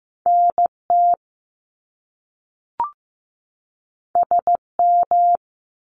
Morse code, heard as a keyed tone, about 15 wpm, 700 Hz. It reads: NT SM
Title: Top Two Letter Combinations in Words 15wpm